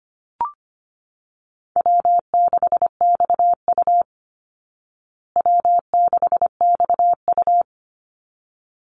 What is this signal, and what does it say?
Morse code, heard as a keyed tone, about 25 wpm, 700 Hz. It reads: W6XU W6XU
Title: Call Signs Repeated 3x - 25wpm